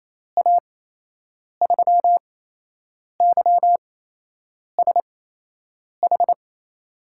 Morse code, heard as a keyed tone, about 28 wpm, spaced as for 8 wpm, 700 Hz. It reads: A3YSH
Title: USA Call Signs - 6 Characters - 8x Character Spacing - 28wpm